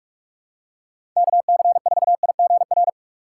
Morse code, heard as a keyed tone, about 45 wpm, 700 Hz. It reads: KC4IGR
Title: USA Call Signs - 6 Characters - 6x Character Spacing - 45wpm